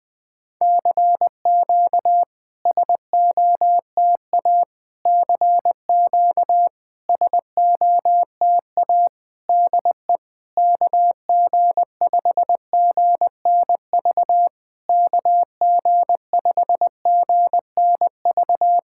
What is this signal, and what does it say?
Morse code, heard as a keyed tone, about 20 wpm, 700 Hz. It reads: CQ SOTA CQ SOTA DE KG5GNV KG5GNV
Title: SOTA QSO Practice 20wpm